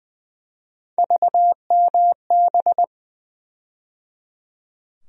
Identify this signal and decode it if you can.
Morse code, heard as a keyed tone, about 21 wpm, 700 Hz. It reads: VMB
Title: USA Call Sign Postfixes - 3 Letters 20wpm